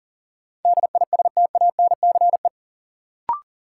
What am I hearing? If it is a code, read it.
Morse code, heard as a keyed tone, about 40 wpm, 700 Hz. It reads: DISTANCE
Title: Sentences from Top 1200 Words - New Word Drill - 40wpm